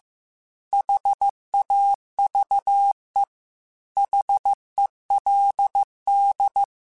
Morse code, heard as a keyed tone, about 15 wpm, 800 Hz. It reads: HAVE HELD
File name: QSOs / QSOs-mp3s / kc7sac@13